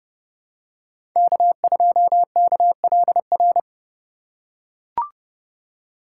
Morse code, heard as a keyed tone, about 30 wpm, 700 Hz. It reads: K2KLR
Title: USA Call Signs - 5 Characters - 4x Character Spacing - 30wpm